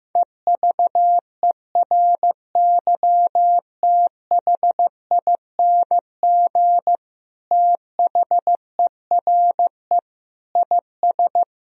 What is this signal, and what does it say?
Morse code, heard as a keyed tone, about 15 wpm, 700 Hz. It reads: EVERYTHING THERE IS
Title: Sentences from Top 1000 Words - Review Part 1 - 15wpm